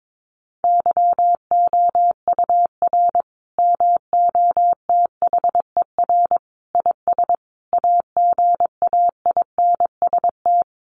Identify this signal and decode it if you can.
Morse code, heard as a keyed tone, about 22 wpm, 700 Hz. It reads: YOUR MOTHER IS AGAINST
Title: Sentences from Top 1400 Words 22wpm